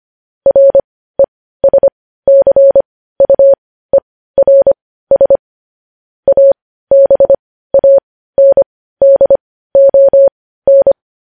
Morse code, heard as a keyed tone, about 25 wpm, spaced as for 15 wpm, 550 Hz. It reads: RESCUERS ABANDON